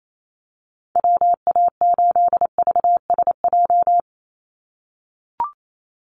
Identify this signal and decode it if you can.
Morse code, heard as a keyed tone, about 28 wpm, 700 Hz. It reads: WA8VSJ
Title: Call Signs - 28wpm